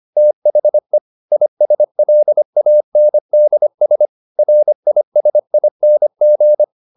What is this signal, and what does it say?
Morse code, heard as a keyed tone, about 25 wpm, 600 Hz. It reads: THE ISLANDS RISING
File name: the old man and the sea part 1 25wpm